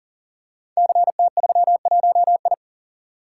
Morse code, heard as a keyed tone, about 42 wpm, 690 Hz. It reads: CT31I
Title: Code Groups - 5 Letter:Numbers - 40wpm